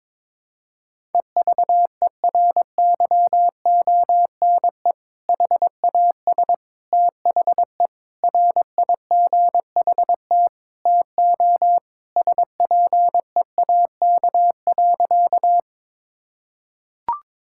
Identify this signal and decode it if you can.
Morse code, heard as a keyed tone, about 22 wpm, 700 Hz. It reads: EVERYONE HAS THE RIGHT TO SPEAK.
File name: practice / 22wpm / Sentences with 3-Word Phrases - 22wpm